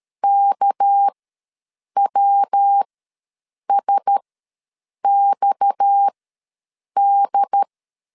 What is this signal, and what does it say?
Morse code, heard as a keyed tone, about 13 wpm, spaced as for 7 wpm, 800 Hz. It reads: KWSXD
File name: LEZIONI / 6 PKX0 / 6 PKX ws 12wpm 3106